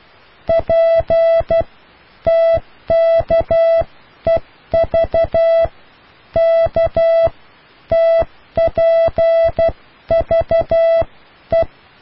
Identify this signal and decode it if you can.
Morse code, heard as a keyed tone, about 12 wpm, spaced as for 10 wpm, 660 Hz. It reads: PTKEV KTPVE